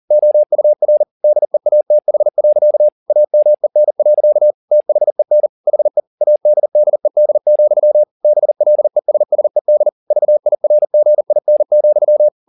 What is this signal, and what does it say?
Morse code, heard as a keyed tone, about 40 wpm, 600 Hz. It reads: OUR DEATH. AMEN. THEN HE ADDED, BLESSED VIRGIN,